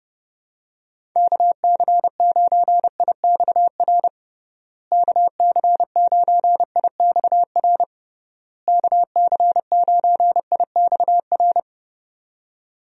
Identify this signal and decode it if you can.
Morse code, heard as a keyed tone, about 30 wpm, 700 Hz. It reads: KC9IXR KC9IXR KC9IXR
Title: USA Call Signs - 6 Characters - Repeated 3x - 30wpm